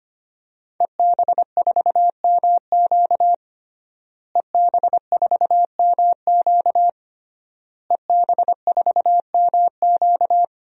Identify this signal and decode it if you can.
Morse code, heard as a keyed tone, about 25 wpm, 700 Hz. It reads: EB4MQ EB4MQ EB4MQ
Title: DXCC-100 Call Signs - 5 Characters - Repeated 3x - 25wpm